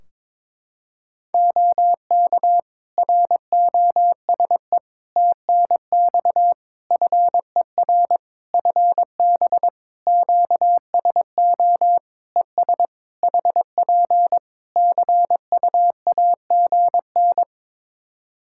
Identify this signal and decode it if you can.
Morse code, heard as a keyed tone, about 22 wpm, 700 Hz. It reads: OK ROSE TNX FER FB QSO ES HP CUAGN
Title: Practice QSO - Exchange 4 - Ending - 22wpm